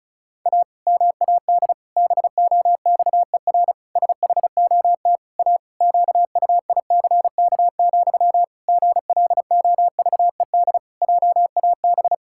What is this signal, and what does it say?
Morse code, heard as a keyed tone, about 36 wpm, 700 Hz. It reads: A MAD BOXER SHOT A QUICK, GLOVED JAB
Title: Pangram Challenge v2 - 35wpm